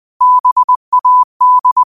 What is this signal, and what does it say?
Morse code, heard as a keyed tone, about 20 wpm, 1000 Hz. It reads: BAD
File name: qotdfiles / 20wpm